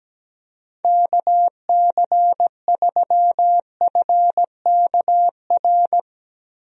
Morse code, heard as a keyed tone, about 17 wpm, 700 Hz. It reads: KC3FKR
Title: USA Call Signs - 6 Characters - Repeated 2x - 17wpm